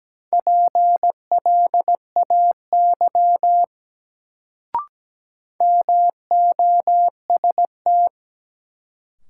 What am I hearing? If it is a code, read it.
Morse code, heard as a keyed tone, about 17 wpm, 700 Hz. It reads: PLAY MOST
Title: Sentences from Top 200 Words - New Word Drill - 17wpm